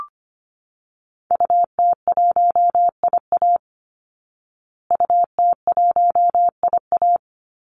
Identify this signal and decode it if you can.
Morse code, heard as a keyed tone, about 25 wpm, 700 Hz. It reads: UT1IA UT1IA
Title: Call Signs Repeated 3x - 25wpm